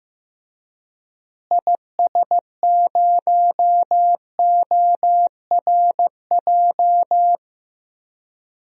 Morse code, heard as a keyed tone, about 15 wpm, 700 Hz. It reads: IS0ORJ
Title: DXCC-100 Call Signs - 6 Characters - Repeated 2x - 15wpm